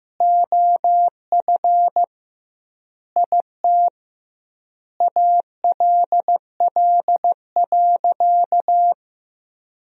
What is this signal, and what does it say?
Morse code, heard as a keyed tone, about 15 wpm, 700 Hz. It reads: OF IT ALL.
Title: 2x Word-Spacing - Sentences from Top 200 Words - 15wpm